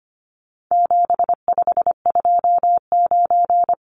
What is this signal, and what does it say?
Morse code, heard as a keyed tone, about 25 wpm, 700 Hz. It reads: 7529
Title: Numbers - 4-Digit 25wpm